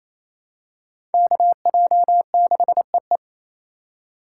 Morse code, heard as a keyed tone, about 28 wpm, 700 Hz. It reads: KJ6EE
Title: USA Call Signs - 5 Characters - 2x Character Spacing - 28wpm